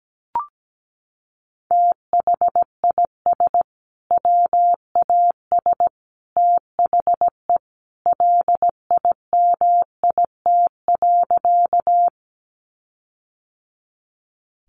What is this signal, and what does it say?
Morse code, heard as a keyed tone, about 17 wpm, 700 Hz. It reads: THIS WAS THE LIMIT.
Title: Sentences from Top 1500 Words 17wpm